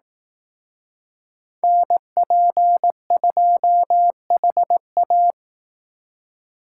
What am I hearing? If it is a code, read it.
Morse code, heard as a keyed tone, about 18 wpm, 700 Hz. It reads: NP2HA
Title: Speed Racing - USA 2x2 Call Signs 15wpm